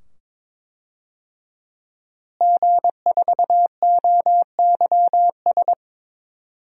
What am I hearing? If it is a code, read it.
Morse code, heard as a keyed tone, about 22 wpm, 700 Hz. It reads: G4OYS
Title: DXCC-100 Call Signs - 5 Characters - Repeated 2x - 22wpm